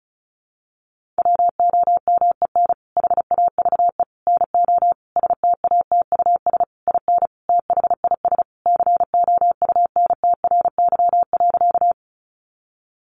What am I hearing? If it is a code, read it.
Morse code, heard as a keyed tone, about 34 wpm, 700 Hz. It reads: WOMEN HAVE NO STATUS IN THIS COUNTRY.
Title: Sentences from Top 1400 Words 35wpm